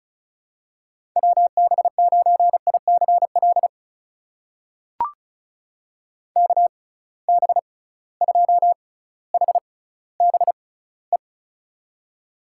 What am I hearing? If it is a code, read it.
Morse code, heard as a keyed tone, about 36 wpm, spaced as for 18 wpm, 700 Hz. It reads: WB9ICL KB2HBE
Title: USA Call Signs - 6 Characters - 6x Character Spacing - 35wpm